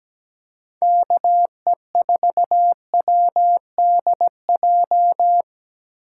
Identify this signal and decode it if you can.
Morse code, heard as a keyed tone, about 20 wpm, 700 Hz. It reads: KE4WDJ
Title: Speed Racing - USA Call Signs 17wpm